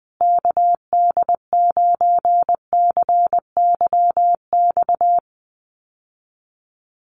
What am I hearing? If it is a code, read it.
Morse code, heard as a keyed tone, about 20 wpm, 700 Hz. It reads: KD9CYX
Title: USA Call Signs Repeated 3x 20wpm